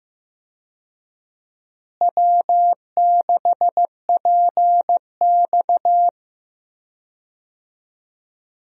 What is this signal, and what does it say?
Morse code, heard as a keyed tone, about 15 wpm, 700 Hz. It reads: W6PX
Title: USA Call Signs - 4 Characters - Repeated 3x - 15wpm